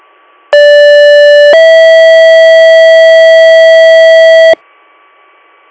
Two-tone page, heard as a one-second tone, then a three-second tone, about 580 then 650 Hz, with noise. It is heard from 0.5 to 4.5 s.